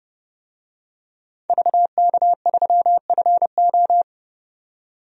Morse code, heard as a keyed tone, about 30 wpm, 700 Hz. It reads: VK3FO
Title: DXCC-100 Call Signs - 5 Characters - 30wpm